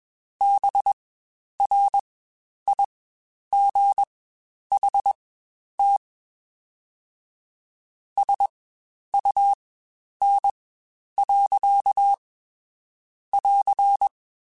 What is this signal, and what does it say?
Morse code, heard as a keyed tone, about 20 wpm, spaced as for 9 wpm, 800 Hz. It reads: BRIGHT SUN.<AR>
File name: Spaced Characters / Spaced Characters-mp3s / Weather5 5 sp @20